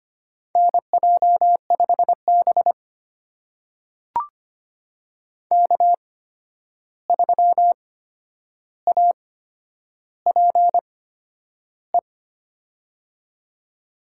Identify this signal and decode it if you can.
Morse code, heard as a keyed tone, about 25 wpm, spaced as for 10 wpm, 700 Hz. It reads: NJ5B K3APE
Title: USA Call Signs - 8x Character Spacing - 25wpm